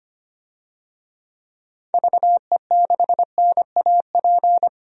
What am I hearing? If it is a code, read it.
Morse code, heard as a keyed tone, about 25 wpm, 700 Hz. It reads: VE6NAP
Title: DXCC-100 Call Signs - 6 Characters - 25wpm